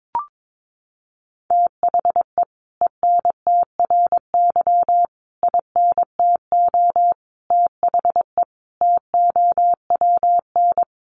Morse code, heard as a keyed tone, about 22 wpm, 700 Hz. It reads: THE ENTRY INTO THE TOWN
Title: Sentences from Top 1500 Words 22wpm